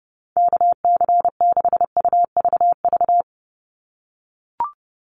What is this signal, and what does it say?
Morse code, heard as a keyed tone, about 30 wpm, 700 Hz. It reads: KC6UVV
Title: USA Call Signs 30wpm